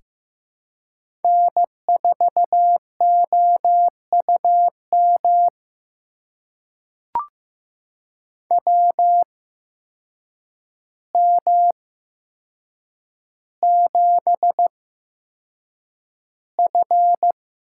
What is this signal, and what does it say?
Morse code, heard as a keyed tone, about 15 wpm, spaced as for 7 wpm, 700 Hz. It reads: N4OUM WM7F
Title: USA Call Signs - 8x Character Spacing - 15wpm